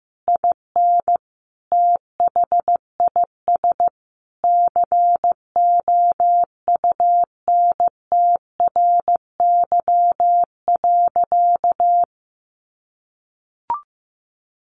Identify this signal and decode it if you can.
Morse code, heard as a keyed tone, about 15 wpm, 700 Hz. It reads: IN THIS COUNTRY.
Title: Sentences from Top 1400 Words 15wpm